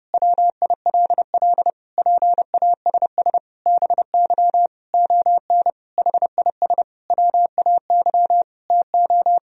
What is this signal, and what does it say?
Morse code, heard as a keyed tone, about 30 wpm, 700 Hz. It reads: WILL PASS BY ON HIS WAY TO
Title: Sentences from Top 1000 Words - Review Part 1 - 30wpm